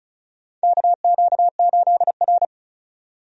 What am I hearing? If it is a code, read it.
Morse code, heard as a keyed tone, about 36 wpm, 700 Hz. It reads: KQ8R
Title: USA Call Signs - 4 Characters - 35wpm